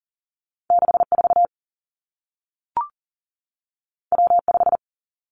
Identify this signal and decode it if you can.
Morse code, heard as a keyed tone, about 40 wpm, 700 Hz. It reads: 64 W5
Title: Two Challenging Letter-Number Combinations 40wpm